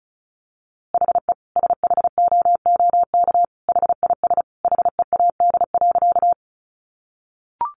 Morse code, heard as a keyed tone, about 34 wpm, 700 Hz. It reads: HE SHOOK HIS HEAD.
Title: A Sound Of Thunder 35wpm